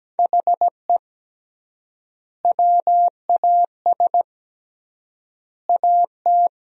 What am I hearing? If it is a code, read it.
Morse code, heard as a keyed tone, about 17 wpm, 700 Hz. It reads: HE WAS AT
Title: 3x Word-Spacing - Sentences from Top 200 Words - 17wpm